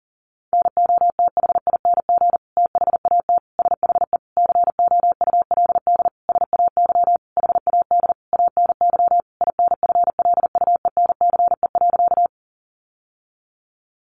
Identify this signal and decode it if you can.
Morse code, heard as a keyed tone, about 40 wpm, 700 Hz. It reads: NOTHING THAT SHE COULD SAY HAD ANY INFLUENCE.
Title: Sentences from Top 1100 Words 40wpm